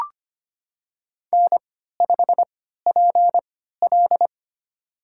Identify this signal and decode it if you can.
Morse code, heard as a keyed tone, about 25 wpm, spaced as for 14 wpm, 700 Hz. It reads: N5PL
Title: USA Call Signs - 4 Characters - 3x Character Spacing - 25wpm